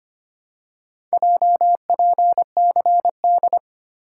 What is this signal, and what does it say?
Morse code, heard as a keyed tone, about 25 wpm, 700 Hz. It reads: JPCD
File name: practice / 25wpm / Four Letter-Number Combinations - 25wpm